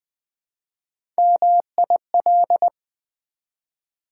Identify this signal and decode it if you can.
Morse code, heard as a keyed tone, about 20 wpm, 700 Hz. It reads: MIL
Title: Build A Word - 1 Letter At A Time - 4-Letter Words - 20wpm